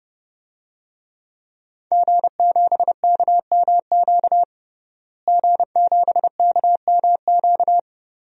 Morse code, heard as a keyed tone, about 32 wpm, 700 Hz. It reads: G7KMQ G7KMQ
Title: DXCC-100 Call Signs - 5 Characters - Repeated 2x - 30wpm